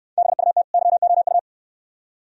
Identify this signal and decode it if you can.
Morse code, heard as a keyed tone, about 69 wpm, 700 Hz. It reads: BUT YOU